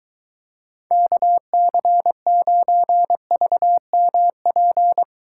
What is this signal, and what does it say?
Morse code, heard as a keyed tone, about 23 wpm, 700 Hz. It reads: KC9VMP